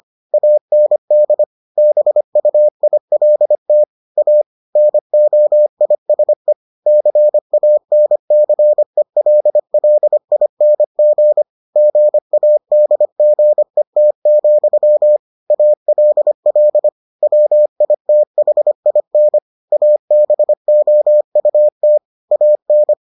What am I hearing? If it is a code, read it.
Morse code, heard as a keyed tone, about 25 wpm, 600 Hz. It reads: AND BUILT A NOISE CANCELLING GADGET, ALL WITHIN ABOUT AN